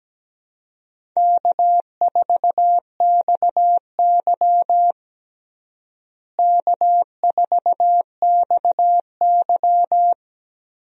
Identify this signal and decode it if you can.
Morse code, heard as a keyed tone, about 17 wpm, 700 Hz. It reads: K4XY K4XY
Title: USA Call Signs - 4 Characters - Repeated 3x - 17wpm